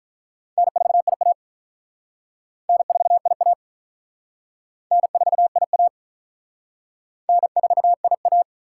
Morse code, heard as a keyed tone, about 47 wpm, 700 Hz. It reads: N4IA N4IA N4IA N4IA